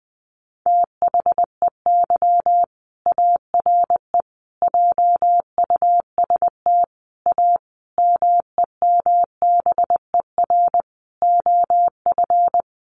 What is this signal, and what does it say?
Morse code, heard as a keyed tone, about 20 wpm, 700 Hz. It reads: THEY ARE JUST A MEMBER OF
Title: Sentences from Top 1100 Words 20wpm